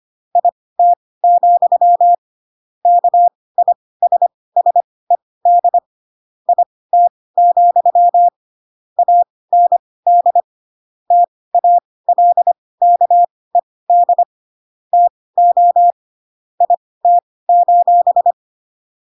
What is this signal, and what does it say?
Morse code, heard as a keyed tone, about 25 wpm, spaced as for 18 wpm, 700 Hz. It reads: IT, KISSED IT, AND TALKED TO IT: